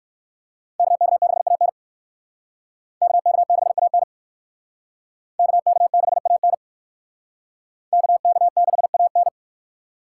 Morse code, heard as a keyed tone, about 62 wpm, 690 Hz. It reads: KK6AN KK6AN KK6AN KK6AN